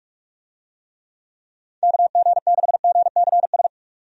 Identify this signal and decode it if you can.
Morse code, heard as a keyed tone, about 46 wpm, 700 Hz. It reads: KG6GCS
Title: USA Call Signs - 6 Characters - 45wpm